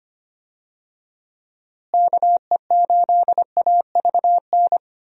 Morse code, heard as a keyed tone, about 25 wpm, 700 Hz. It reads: KE8AVN
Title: USA Call Signs - 6 Characters - Repeated 3x - 25wpm